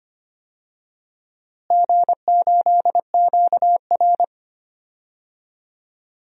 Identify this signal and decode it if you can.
Morse code, heard as a keyed tone, about 25 wpm, 700 Hz. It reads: G8QR
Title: DXCC-100 Call Signs - 4 Characters - Repeated 3x - 25wpm